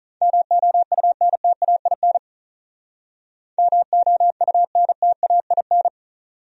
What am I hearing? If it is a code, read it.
Morse code, heard as a keyed tone, about 43 wpm, 700 Hz. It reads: MOUNTAIN MOUNTAIN
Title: Speed-Racing Top 500 Words 35wpm